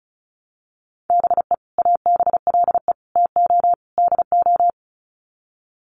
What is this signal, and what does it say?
Morse code, heard as a keyed tone, about 34 wpm, 700 Hz. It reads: BE ABLE TO DO
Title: Sets of 4 Words 35wpm